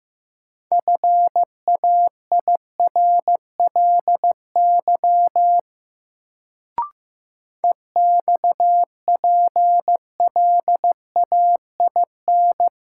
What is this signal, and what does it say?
Morse code, heard as a keyed tone, about 15 wpm, 700 Hz. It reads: FAIRLY EXPLAIN
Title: Sentences from Top 1100 Words - New Word Drill - 15wpm